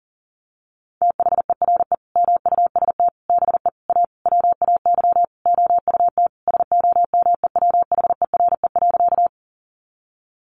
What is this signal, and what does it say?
Morse code, heard as a keyed tone, about 40 wpm, 700 Hz. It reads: THERE MUST BE A WAY OUT SOMEWHERE.
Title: Sentences from Top 1400 Words 40wpm